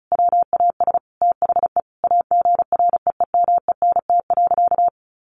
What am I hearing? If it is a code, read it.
Morse code, heard as a keyed tone, about 34 wpm, 700 Hz. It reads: WAS THE AGREEMENT.
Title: Sentences from Top 1500 Words 35wpm